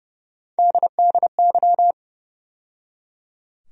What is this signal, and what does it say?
Morse code, heard as a keyed tone, about 30 wpm, 700 Hz. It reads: DDY
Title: USA Call Sign Postfixes - 3 Letters 30wpm